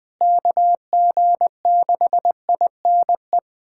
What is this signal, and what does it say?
Morse code, heard as a keyed tone, about 20 wpm, 700 Hz. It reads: KG6INE
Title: USA Callsign DE Callsign 20wpm